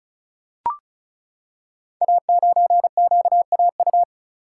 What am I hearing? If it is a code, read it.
Morse code, heard as a keyed tone, about 36 wpm, 700 Hz. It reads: A9QAU
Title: Code Groups - 5 Letter:Numbers - 35wpm